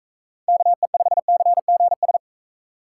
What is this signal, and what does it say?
Morse code, heard as a keyed tone, about 42 wpm, 700 Hz. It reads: KE5CGS